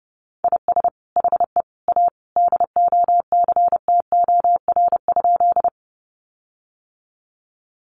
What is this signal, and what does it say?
Morse code, heard as a keyed tone, about 30 wpm, 700 Hz. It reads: IS HE A DOCTOR?
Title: Sentences from Top 1500 Words 30wpm